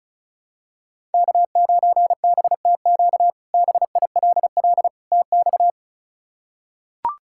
Morse code, heard as a keyed tone, about 36 wpm, 700 Hz. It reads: K9BTQ BILL TX
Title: K1USN SST QSO Practice - 35wpm